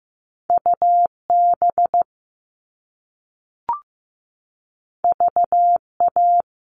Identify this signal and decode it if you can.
Morse code, heard as a keyed tone, about 15 wpm, 700 Hz. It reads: UB VA
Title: All Cut Numbers - 2-Digit 15wpm